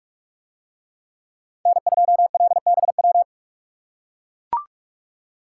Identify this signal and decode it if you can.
Morse code, heard as a keyed tone, about 44 wpm, 700 Hz. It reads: N2LBW